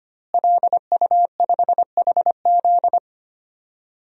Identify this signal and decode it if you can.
Morse code, heard as a keyed tone, about 25 wpm, 700 Hz. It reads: LU5HZ
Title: DXCC-100 Call Signs - 5 Characters - Repeated 2x - 25wpm